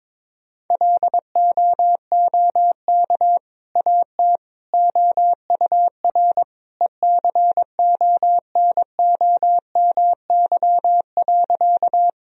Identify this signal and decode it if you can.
Morse code, heard as a keyed tone, about 21 wpm, 700 Hz. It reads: LOOK AT OUR ECONOMY.
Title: Sentences from Top 1000 Words - Review Part 3 - 22wpm